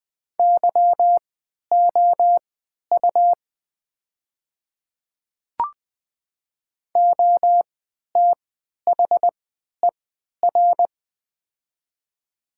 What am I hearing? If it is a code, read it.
Morse code, heard as a keyed tone, about 20 wpm, spaced as for 11 wpm, 700 Hz. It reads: YOU OTHER
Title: 3x Character Spacing - Top 100 Words - 20wpm